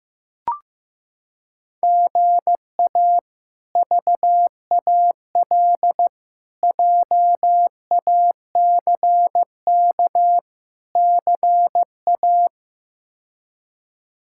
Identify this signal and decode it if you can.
Morse code, heard as a keyed tone, about 15 wpm, 700 Hz. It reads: GA VAL JACK CA
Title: K1USN SST QSO Practice - 15wpm